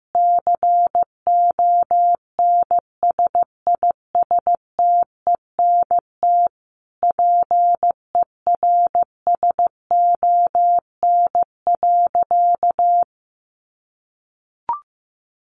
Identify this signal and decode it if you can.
Morse code, heard as a keyed tone, about 15 wpm, 700 Hz. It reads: CONSISTENT PERSON.
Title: Sentences from Top 1500 Words 15wpm